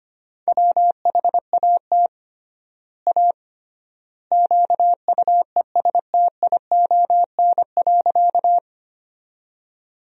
Morse code, heard as a keyed tone, about 25 wpm, 700 Hz. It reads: WHAT A QUESTION.
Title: 3x Word-Spacing - Sentences from Top 400 Words - 25wpm